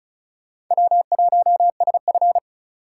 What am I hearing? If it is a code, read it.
Morse code, heard as a keyed tone, about 47 wpm, 700 Hz. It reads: W1SF